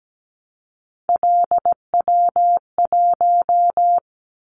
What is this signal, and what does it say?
Morse code, heard as a keyed tone, about 17 wpm, 700 Hz. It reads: LW1
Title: Callsign Prefixes for Top 50 DXCC - 17wpm